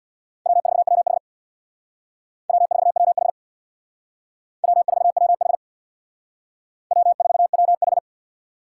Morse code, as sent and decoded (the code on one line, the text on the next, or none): .-- ....- .-- .... / .-- ....- .-- .... / .-- ....- .-- .... / .-- ....- .-- ....
W4WH W4WH W4WH W4WH